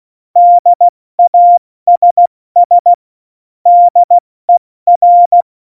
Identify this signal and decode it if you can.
Morse code, heard as a keyed tone, about 16 wpm, 700 Hz. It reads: DASS DER
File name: Morsekurs / Witze / Witz 27. November 2018, 70 F80